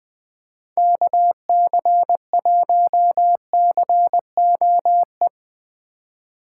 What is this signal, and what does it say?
Morse code, heard as a keyed tone, about 21 wpm, 700 Hz. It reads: KC1COE
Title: Speed Racing - USA Call Signs 20wpm